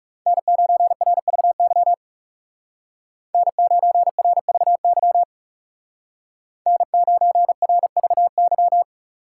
Morse code, heard as a keyed tone, about 35 wpm, 700 Hz. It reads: N9RVY N9RVY N9RVY
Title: Speed Racing - USA 1x3 Call Signs 30wpm